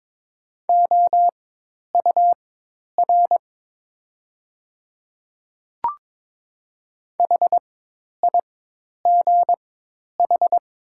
Morse code, heard as a keyed tone, about 22 wpm, spaced as for 10 wpm, 700 Hz. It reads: OUR HIGH
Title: 4x Character Spacing - Top 300 Words - 22wpm